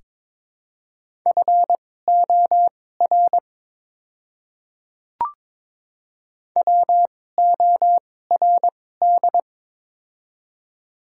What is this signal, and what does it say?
Morse code, heard as a keyed tone, about 22 wpm, spaced as for 16 wpm, 700 Hz. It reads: FOR WORD
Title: 2x Character Spacing - Top 100 Words - 22wpm